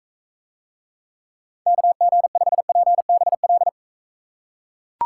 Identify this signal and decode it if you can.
Morse code, heard as a keyed tone, about 42 wpm, 700 Hz. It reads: KG5PBL